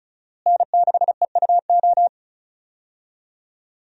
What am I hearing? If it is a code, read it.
Morse code, heard as a keyed tone, about 36 wpm, 700 Hz. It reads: N6EUO
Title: USA Call Signs - 5 Characters - Repeated 3x - 35wpm